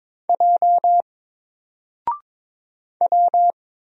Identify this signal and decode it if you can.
Morse code, heard as a keyed tone, about 22 wpm, 700 Hz. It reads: J W
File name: Single-Letter-Number-22wpm